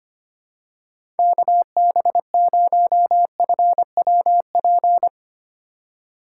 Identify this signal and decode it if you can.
Morse code, heard as a keyed tone, about 25 wpm, 700 Hz. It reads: KB0FWP
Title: USA Call Signs - 6 Characters - 2x Character Spacing - 25wpm